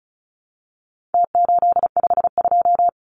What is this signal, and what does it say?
Morse code, heard as a keyed tone, about 34 wpm, 700 Hz. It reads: T852
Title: Numbers - Contest Style - 4-Digit 35wpm